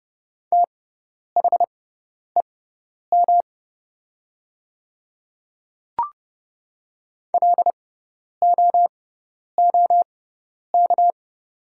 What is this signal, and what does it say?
Morse code, heard as a keyed tone, about 30 wpm, spaced as for 10 wpm, 700 Hz. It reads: THEM LOOK
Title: 6x Character Spacing - Top 200 Words - 30wpm